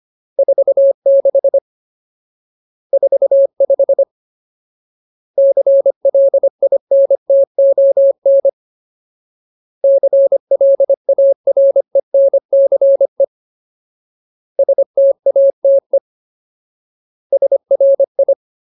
Morse code, heard as a keyed tone, about 25 wpm, 550 Hz. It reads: 46 45 CLINTON CLARENCE STATE SRI
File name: Ultimate Word List / 25WPM Ultimate